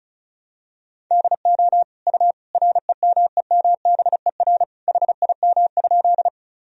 Morse code, heard as a keyed tone, about 36 wpm, 700 Hz. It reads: DO U REMEMBER HIM?
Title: Sentences from Top 400 Words and Abbreviations 35wpm